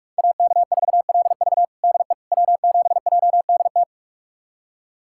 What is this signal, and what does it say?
Morse code, heard as a keyed tone, about 42 wpm, 700 Hz. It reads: AK4LV DE W7JDT
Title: USA Callsign DE Callsign - 5 Characters - 45wpm